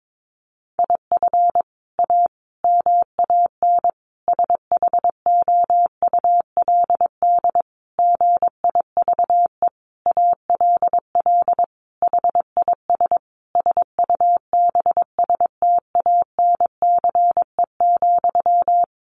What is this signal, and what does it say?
Morse code, heard as a keyed tone, about 22 wpm, 700 Hz. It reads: IF A MAN SHOULD GIVE ALL HIS SUBSTANCE,
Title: Sentences from Top 1500 Words 22wpm